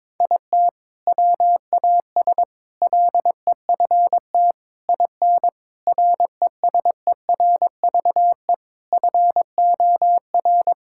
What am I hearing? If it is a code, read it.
Morse code, heard as a keyed tone, about 22 wpm, 700 Hz. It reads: IT WAS LEFT IN RESERVE FOR
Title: Sentences from Top 1000 Words - Review Part 6 - 22wpm